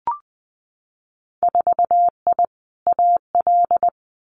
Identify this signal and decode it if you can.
Morse code, heard as a keyed tone, about 20 wpm, 700 Hz. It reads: 4I AL
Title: Winter Field Day Exchanges Encore 20wpm